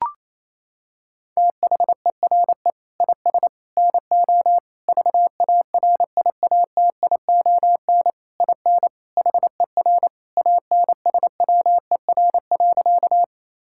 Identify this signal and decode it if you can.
Morse code, heard as a keyed tone, about 28 wpm, 700 Hz. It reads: THERE IS NO VARIATION IN HER ANSWER.
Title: Sentences from Top 1600 Words 28wpm